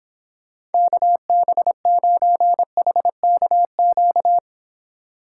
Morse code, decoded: KB9HKQ

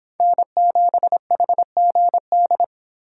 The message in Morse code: N7HGD

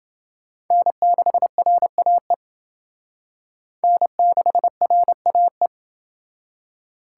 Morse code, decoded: N6RAE N6RAE

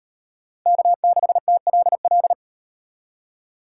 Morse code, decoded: K6TLL

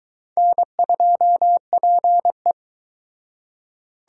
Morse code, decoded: N2PE

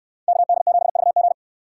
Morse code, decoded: KD7VG

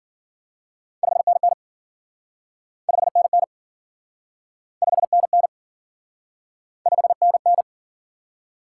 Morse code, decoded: 5NN 5NN 5NN 5NN

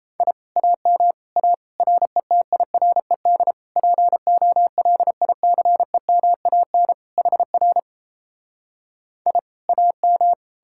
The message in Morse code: I AM A RETIRED POLICEMAN HR I AM

